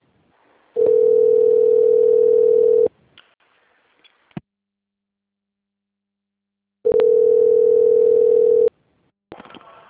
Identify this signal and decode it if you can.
Ringback tone (North America), 440 + 480 Hz.